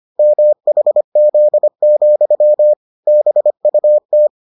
Morse code, heard as a keyed tone, about 25 wpm, 600 Hz. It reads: MHZ, BUT